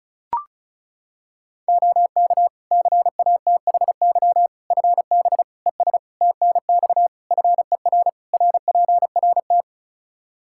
Morse code, heard as a keyed tone, about 34 wpm, 700 Hz. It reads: OK CATHY FB ES TNX FER RPRT
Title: K1USN SST QSO Practice 35wpm